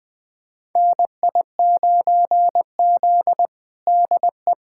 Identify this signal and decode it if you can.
Morse code, heard as a keyed tone, about 20 wpm, 700 Hz. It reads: NI9Z DE
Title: USA Callsign DE Callsign - 4 Characters - 20wpm